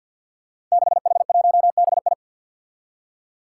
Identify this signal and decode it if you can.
Morse code, heard as a keyed tone, about 50 wpm, 700 Hz. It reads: 6H1BI
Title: Five Challenging Letter-Number Combinations 50wpm